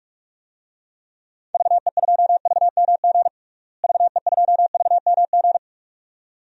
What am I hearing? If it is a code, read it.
Morse code, heard as a keyed tone, about 46 wpm, 700 Hz. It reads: VE2VMG VE2VMG
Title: DXCC-100 Call Signs - 6 Characters - Repeated 2x - 45wpm